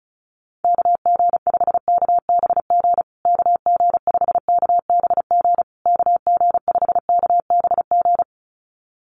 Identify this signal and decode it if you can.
Morse code, heard as a keyed tone, about 34 wpm, 700 Hz. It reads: KG5KBG KG5KBG KG5KBG
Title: USA Call Signs Repeated 3x 35wpm